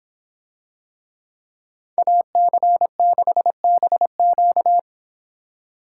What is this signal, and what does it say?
Morse code, heard as a keyed tone, about 25 wpm, 700 Hz. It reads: AC6BQ